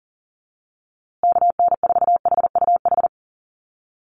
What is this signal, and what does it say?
Morse code, heard as a keyed tone, about 40 wpm, 700 Hz. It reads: KN4HUH
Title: USA Call Signs Repeated 3x 40wpm